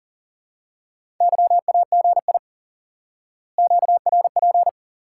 Morse code, heard as a keyed tone, about 40 wpm, 700 Hz. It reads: YAGI QRP
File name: IQE Course 3 - Lesson 89 - YAGI - Two Elements - 40wpm